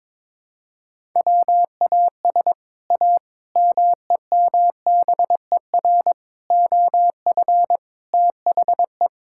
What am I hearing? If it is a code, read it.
Morse code, heard as a keyed tone, about 21 wpm, 700 Hz. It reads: WAS A MEMBER OF THE
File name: practice / 22wpm / Build A Phrase - 5-Word Phrases - 22wpm